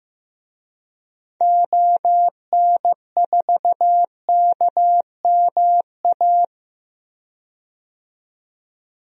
Morse code, heard as a keyed tone, about 15 wpm, 700 Hz. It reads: ON4KMA
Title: DXCC-100 Call Signs - 6 Characters - 15wpm